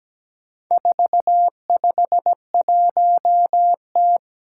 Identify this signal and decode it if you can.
Morse code, heard as a keyed tone, about 17 wpm, 700 Hz. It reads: 451T
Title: Four Challenging Letter-Number Combinations 17wpm